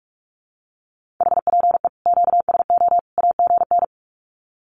Morse code, heard as a keyed tone, about 44 wpm, 700 Hz. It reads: HPE QSO AGN
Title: Common QSO Phrases 45wpm